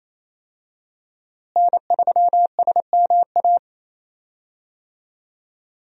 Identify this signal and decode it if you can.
Morse code, heard as a keyed tone, about 28 wpm, 700 Hz. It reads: N3SMA
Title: USA Call Signs - 5 Characters - Repeated 2x - 28wpm